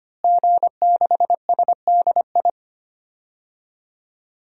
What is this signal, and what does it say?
Morse code, heard as a keyed tone, about 25 wpm, 700 Hz. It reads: G6SDI
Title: DXCC-100 Call Signs - 5 Characters - Repeated 2x - 25wpm